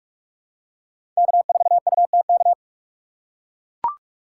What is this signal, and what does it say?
Morse code, heard as a keyed tone, about 44 wpm, 690 Hz. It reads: K4UTK
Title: USA Call Signs - 5 Characters - 6x Character Spacing - 45wpm